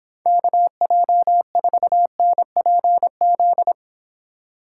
Morse code, heard as a keyed tone, about 26 wpm, 700 Hz. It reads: KJ4NPZ